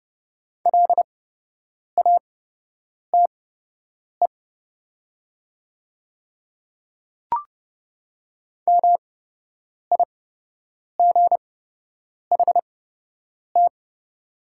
Morse code, heard as a keyed tone, about 30 wpm, spaced as for 8 wpm, 700 Hz. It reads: LATE MIGHT
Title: 8x Character Spacing - Top 300 Words - 30wpm